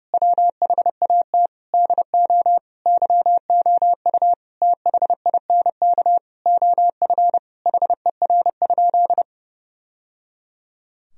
Morse code, heard as a keyed tone, about 30 wpm, 700 Hz. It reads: WHAT DO YOU THINK OF HER?
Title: Sentences from Top 1000 Words - Review Part 4 - 30wpm